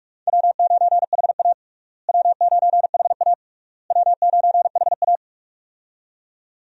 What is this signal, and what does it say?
Morse code, heard as a keyed tone, about 46 wpm, 700 Hz. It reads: W9HA W9HA W9HA
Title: USA Call Signs - 4 Characters - Repeated 3x - 45wpm